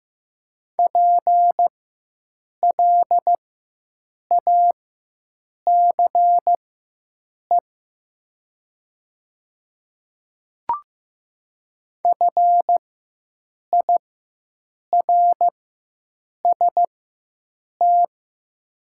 Morse code, heard as a keyed tone, about 15 wpm, spaced as for 7 wpm, 700 Hz. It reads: PLACE FIRST
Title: 4x Character Spacing - Top 200 Words - 15wpm